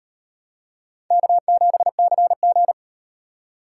Morse code, heard as a keyed tone, about 38 wpm, 700 Hz. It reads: K7CG